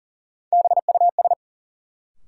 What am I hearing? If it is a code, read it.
Morse code, heard as a keyed tone, about 40 wpm, 700 Hz. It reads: BUS
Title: Sentences from Top 1000 Words - Review Part 6 - New Word Drill - 40wpm